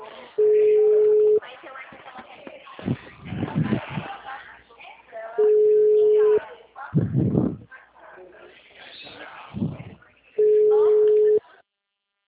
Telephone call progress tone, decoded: ringback tone

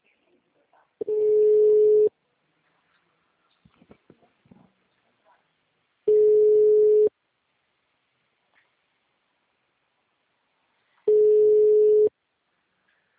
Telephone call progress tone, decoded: ringback tone

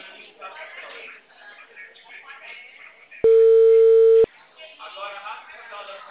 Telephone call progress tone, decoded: ringback tone

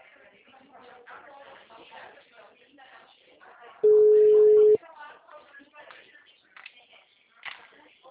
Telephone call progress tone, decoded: ringback tone

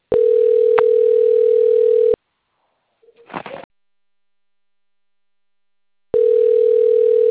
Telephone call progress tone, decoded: ringback tone

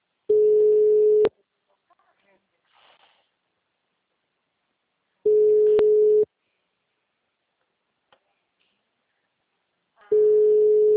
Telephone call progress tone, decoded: ringback tone

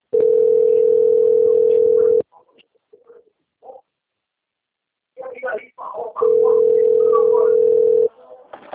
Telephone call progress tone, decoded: ringback tone

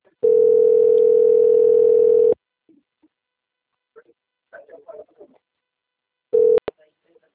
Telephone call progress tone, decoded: ringback tone

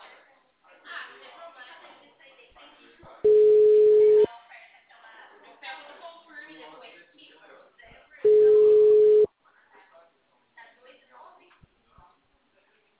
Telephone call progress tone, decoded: ringback tone

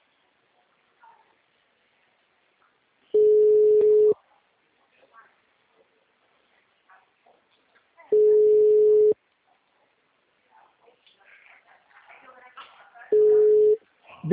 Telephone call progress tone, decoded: ringback tone

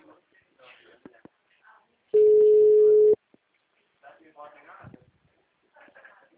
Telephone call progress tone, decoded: ringback tone

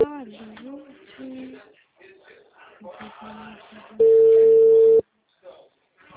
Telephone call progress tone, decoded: ringback tone

